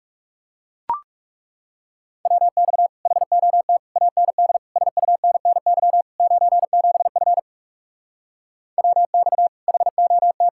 Morse code, heard as a keyed tone, about 40 wpm, 700 Hz. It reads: WX HOT AND SUNNY 97F WX HOT